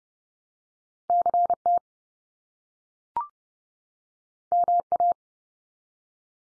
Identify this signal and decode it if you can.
Morse code, heard as a keyed tone, about 30 wpm, 700 Hz. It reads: CT MA
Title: US States 30WPM